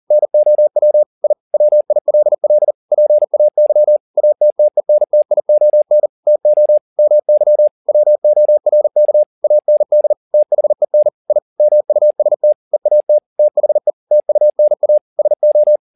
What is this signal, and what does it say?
Morse code, heard as a keyed tone, about 40 wpm, 600 Hz. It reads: NOW I WILL PAY ATTENTION TO MY WORK AND THEN I MUST EAT THE TUNA SO